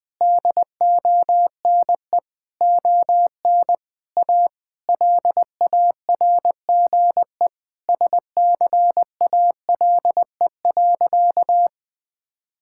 Morse code, heard as a keyed tone, about 20 wpm, 700 Hz. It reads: DONE ON A LARGE SCALE.
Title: Sentences from Top 1000 Words - Review Part 5 - 20wpm